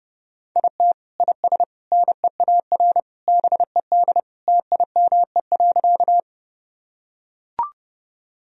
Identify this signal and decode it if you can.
Morse code, heard as a keyed tone, about 30 wpm, 700 Hz. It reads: IT IS NEAR BED TIME.
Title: Sentences from Top 1000 Words - Review Part 1 - 30wpm